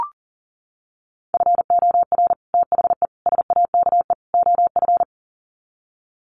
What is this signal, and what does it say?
Morse code, heard as a keyed tone, about 37 wpm, 700 Hz. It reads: FOR THE SAKE OF
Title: Sets of 4 Words 40wpm